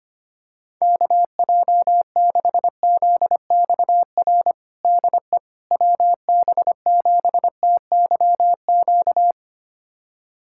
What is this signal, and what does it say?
Morse code, heard as a keyed tone, about 25 wpm, 700 Hz. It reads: KJ6ZXR DE WB7TYQ
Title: USA Callsign DE Callsign - 6 Characters - 25wpm